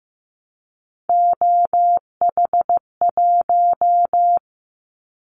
Morse code, heard as a keyed tone, about 15 wpm, 700 Hz. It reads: OH1
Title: Callsign Prefixes for Top 50 DXCC - 15wpm